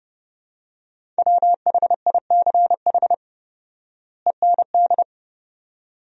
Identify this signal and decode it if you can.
Morse code, heard as a keyed tone, about 30 wpm, 700 Hz. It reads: WHICH END
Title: 4x Word-Spacing - Sentences from Top 300 Words - 30wpm